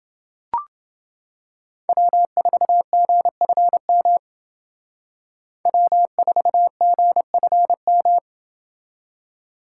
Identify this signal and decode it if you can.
Morse code, heard as a keyed tone, about 28 wpm, 700 Hz. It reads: W4GFM W4GFM